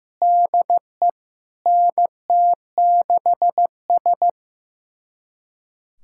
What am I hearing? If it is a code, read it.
Morse code, heard as a keyed tone, about 15 wpm, 700 Hz. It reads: DE NT6S
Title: USA Callsign DE Callsign - 4 Characters - 15wpm